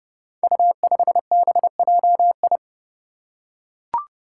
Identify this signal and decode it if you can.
Morse code, heard as a keyed tone, about 30 wpm, 700 Hz. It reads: U5BJI
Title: Five Challenging Letter-Number Combinations 30wpm